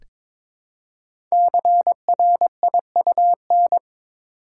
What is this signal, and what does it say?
Morse code, heard as a keyed tone, about 22 wpm, 700 Hz. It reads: CRIUN
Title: Code Groups - 5 Letters - 22wpm